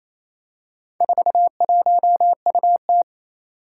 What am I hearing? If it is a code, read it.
Morse code, heard as a keyed tone, about 28 wpm, 700 Hz. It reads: 41UT
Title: Four Challenging Letter-Number Combinations 28wpm